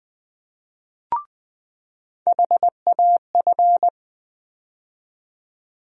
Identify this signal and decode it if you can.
Morse code, heard as a keyed tone, about 20 wpm, 700 Hz. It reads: HAF
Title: USA Call Sign Postfixes - 3 Letters 20wpm